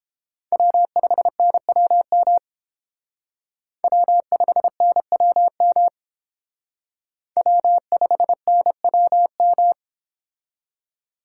Morse code, heard as a keyed tone, about 30 wpm, 700 Hz. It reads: W5NWM W5NWM W5NWM